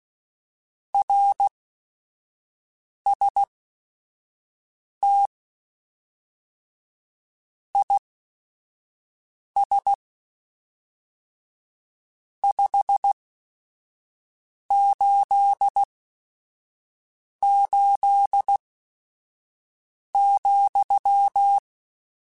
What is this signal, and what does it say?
Morse code, heard as a keyed tone, about 16 wpm, spaced as for 5 wpm, 800 Hz. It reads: RST IS 588,
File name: QSOs / QSOs-mp3s / WA7QOX 1@5